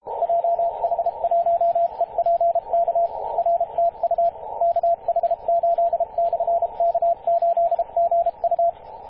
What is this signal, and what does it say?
Morse code, heard as a keyed tone, about 31 wpm, 680 Hz, with radio noise. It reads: JH1EPK R TU KH8/K8GU
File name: posts / 2016-apr-29-0441z-kh8-k8gu-18-069mhz-cw